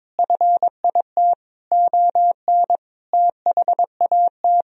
Morse code, heard as a keyed tone, about 22 wpm, 700 Hz. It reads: FIT ON THAT
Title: Sentences from Top 1000 Words - Review Part 4 - 22wpm